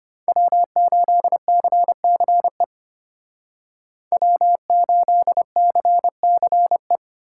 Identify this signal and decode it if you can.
Morse code, heard as a keyed tone, about 33 wpm, 700 Hz. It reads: W8CCE W8CCE